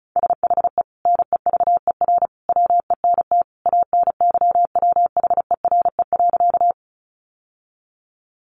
Morse code, heard as a keyed tone, about 34 wpm, 700 Hz. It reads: SHE NEVER WENT ANYWHERE.
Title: Sentences from Top 1400 Words 35wpm